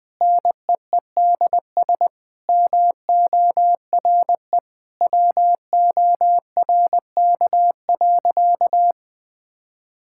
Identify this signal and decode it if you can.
Morse code, heard as a keyed tone, about 21 wpm, 700 Hz. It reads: NEEDS MORE WORK.
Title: Sentences with 8-Word Phrases - 20wpm